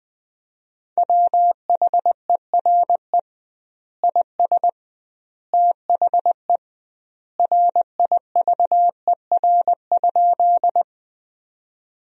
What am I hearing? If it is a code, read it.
Morse code, heard as a keyed tone, about 20 wpm, 700 Hz. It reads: WHERE IS THE RIVER?
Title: 2x Word-Spacing - Sentences from Top 300 Words - 20wpm